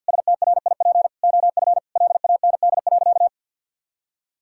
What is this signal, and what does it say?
Morse code, heard as a keyed tone, about 46 wpm, 700 Hz. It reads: STRIP OF LAND.
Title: Sentences from Top 1600 Words 50wpm